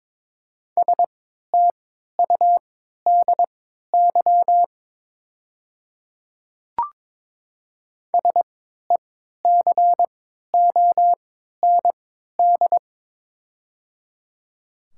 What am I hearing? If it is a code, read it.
Morse code, heard as a keyed tone, about 22 wpm, spaced as for 12 wpm, 700 Hz. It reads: STUDY SECOND
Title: 3x Character Spacing - Top 500 Words - 22wpm